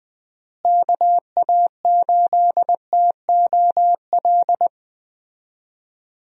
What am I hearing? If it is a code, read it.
Morse code, heard as a keyed tone, about 21 wpm, 700 Hz. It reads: KA8TOL
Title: USA Call Signs - 6 Characters - Repeated 3x - 20wpm